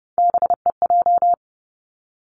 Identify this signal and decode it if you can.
Morse code, heard as a keyed tone, about 30 wpm, 700 Hz. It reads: BEJ
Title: Three Challenging Letter-Number Combinations 30wpm